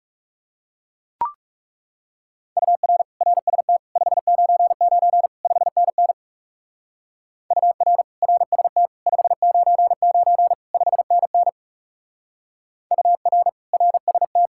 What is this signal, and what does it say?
Morse code, heard as a keyed tone, about 37 wpm, 700 Hz. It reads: UR RST 599 5NN UR RST 599 5NN UR RST
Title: Speed-Racing Common QSO Phrases - 30wpm